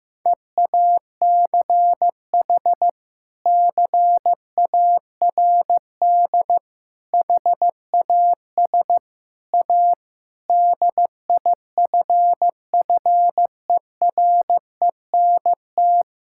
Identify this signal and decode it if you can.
Morse code, heard as a keyed tone, about 15 wpm, 700 Hz. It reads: EACH CARD HAS A DIFFERENT
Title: Sentences from Top 1000 Words - Review Part 6 - 15wpm